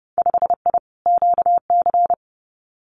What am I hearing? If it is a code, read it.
Morse code, heard as a keyed tone, about 30 wpm, 700 Hz. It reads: 5I QC
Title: Winter Field Day Exchanges Encore 30wpm